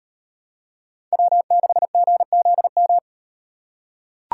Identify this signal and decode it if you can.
Morse code, heard as a keyed tone, about 36 wpm, 700 Hz. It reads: W6GZM